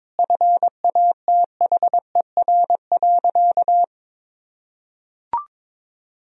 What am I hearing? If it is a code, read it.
Morse code, heard as a keyed tone, about 22 wpm, 700 Hz. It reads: FATHER.
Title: Sentences from Top 500 Words and Abbreviations 22wpm